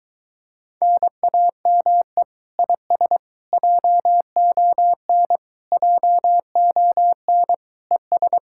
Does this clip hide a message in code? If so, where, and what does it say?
Yes: Morse code from 0.8 to 8.4 s, heard as a keyed tone, about 23 wpm, 700 Hz. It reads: NAME IS JON JON ES